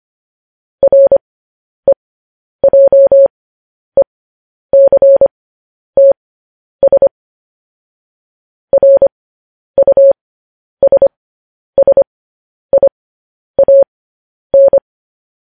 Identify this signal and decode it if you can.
Morse code, heard as a keyed tone, about 25 wpm, spaced as for 10 wpm, 550 Hz. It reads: REJECTS RUSSIAN
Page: News Headlines in Morse Code at 10 WPM